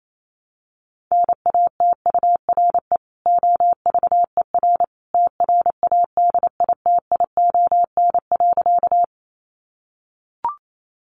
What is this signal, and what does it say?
Morse code, heard as a keyed tone, about 28 wpm, 700 Hz. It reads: NATURE OVER TRADITION.
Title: Sentences from Top 1500 Words 28wpm